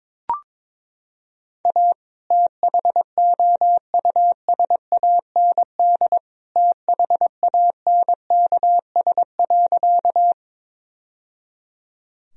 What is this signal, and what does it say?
Morse code, heard as a keyed tone, about 22 wpm, 700 Hz. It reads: A THOUSAND THANKS.
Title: Sentences from Top 1000 Words - Review Part 3 - 22wpm